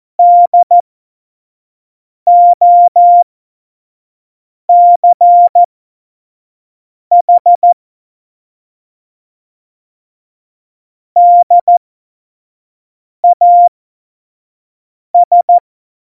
Morse code, heard as a keyed tone, about 14 wpm, spaced as for 5 wpm, 700 Hz. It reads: DOCH DAS
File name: Morsekurs / Witze / Witz 17.Juli 2018, 25 F70